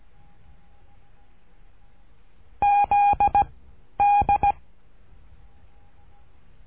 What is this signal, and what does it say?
Morse code, heard as a keyed tone, about 17 wpm, 810 Hz. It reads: Z D
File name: cursos / CURSO DE TELEGRAFIA / Telegrafia - Trilha 13